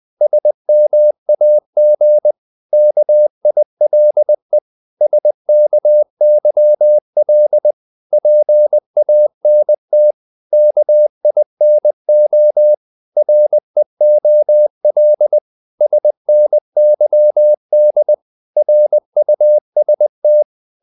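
Morse code, heard as a keyed tone, about 20 wpm, 600 Hz. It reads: SMAG KILE SKYL PANT KINO REOL SNYD RUST
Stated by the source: Morse code practice of short Danish words at 20 wpm, from CW med Gnister.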